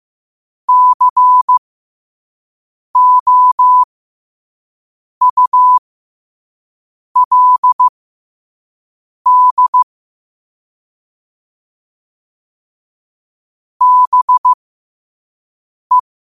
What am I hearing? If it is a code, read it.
Morse code, heard as a keyed tone, about 15 wpm, spaced as for 5 wpm, 1000 Hz. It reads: COULD BE
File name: qotdfiles / 5wpm